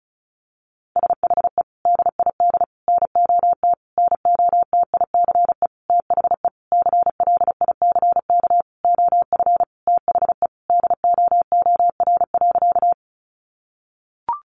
Morse code, heard as a keyed tone, about 34 wpm, 700 Hz. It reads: SHE DID NOT NOTICE THE CLICK OF THE DOOR.
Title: Sentences from Top 1100 Words 35wpm